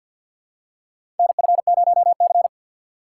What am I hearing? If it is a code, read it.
Morse code, heard as a keyed tone, about 50 wpm, 700 Hz. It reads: NF0C
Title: USA Call Signs - 4 Characters - 2x Character Spacing - 50wpm